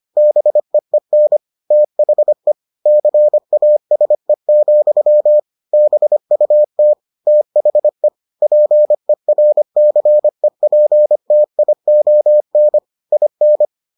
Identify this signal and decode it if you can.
Morse code, heard as a keyed tone, about 25 wpm, 600 Hz. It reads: BEEN THE CASE, BUT THE PERCEPTION IN